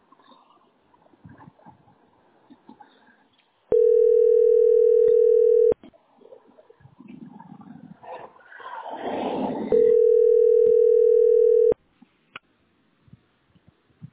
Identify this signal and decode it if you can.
Ringback tone (North America), 440 + 480 Hz.